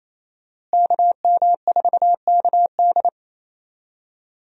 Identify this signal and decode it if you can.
Morse code, heard as a keyed tone, about 27 wpm, 700 Hz. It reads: KM4KD
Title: USA Call Signs - 5 Characters - Repeated 3x - 28wpm